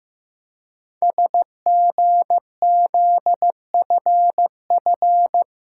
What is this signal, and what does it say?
Morse code, heard as a keyed tone, about 15 wpm, 700 Hz. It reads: SGZFF
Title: Code Groups - 5 Letters - 15wpm